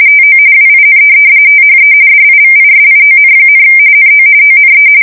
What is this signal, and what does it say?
RTTY, about 45.45 baud. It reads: TO WIN THE DIVISION, NO ONE
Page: 170 Shift 2125/2295